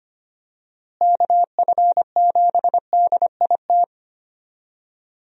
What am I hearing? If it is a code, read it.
Morse code, heard as a keyed tone, about 25 wpm, 700 Hz. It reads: KF7DIT
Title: USA Call Signs - 6 Characters - Repeated 3x - 25wpm